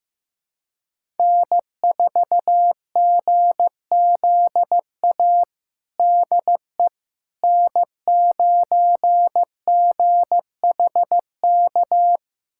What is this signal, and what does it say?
Morse code, heard as a keyed tone, about 15 wpm, 700 Hz. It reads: N4GZA DE N9GHK
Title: USA Callsign DE Callsign - 5 Characters - 15wpm